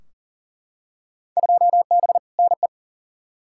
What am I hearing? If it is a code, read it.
Morse code, heard as a keyed tone, about 39 wpm, 700 Hz. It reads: 2B NE
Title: Field Day Exchanges Encore - 40wpm